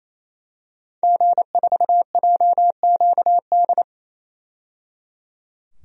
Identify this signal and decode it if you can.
Morse code, heard as a keyed tone, about 28 wpm, 700 Hz. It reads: G4JQD
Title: DXCC-100 Call Signs - 5 Characters - 28wpm